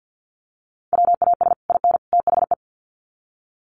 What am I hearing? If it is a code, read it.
Morse code, heard as a keyed tone, about 50 wpm, 700 Hz. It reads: WAS IN THE
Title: Sets of 3 Words 50wpm